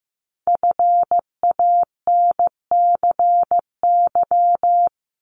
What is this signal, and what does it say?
Morse code, heard as a keyed tone, about 15 wpm, 700 Hz. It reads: FANCY
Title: Multi-letter Phonemes 15wpm